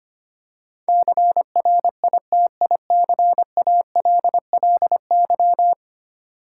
Morse code, heard as a keyed tone, about 25 wpm, 700 Hz. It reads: CRITICALLY
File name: 10 Letter Words - First 3 Letters Given - 25wpm